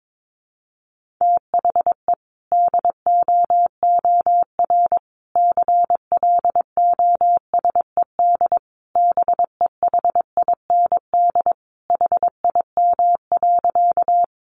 Morse code, heard as a keyed tone, about 22 wpm, 700 Hz. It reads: THE DOOR CLOSED BEHIND HIM.
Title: Sentences from Top 1400 Words 22wpm